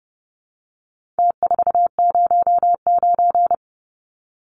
Morse code, heard as a keyed tone, about 30 wpm, 700 Hz. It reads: T409
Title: Numbers - Contest Style - 4-Digit 30wpm